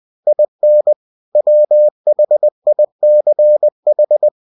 Morse code, heard as a keyed tone, about 20 wpm, 600 Hz. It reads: IN WHICH